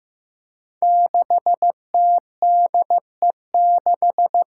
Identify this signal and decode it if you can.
Morse code, heard as a keyed tone, about 15 wpm, 700 Hz. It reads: 6TDE6
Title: Five Challenging Letter-Number Combinations 15wpm